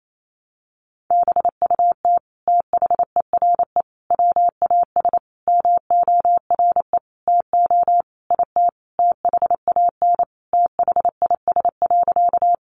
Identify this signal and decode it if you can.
Morse code, heard as a keyed tone, about 28 wpm, 700 Hz. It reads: BUT THERE WAS MORE TO IT THAN THIS.
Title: Sentences from Top 100 Words 28wpm